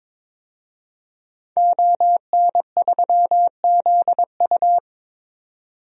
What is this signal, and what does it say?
Morse code, heard as a keyed tone, about 22 wpm, 700 Hz. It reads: ON3ZU
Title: DXCC-100 Call Signs - 5 Characters - 22wpm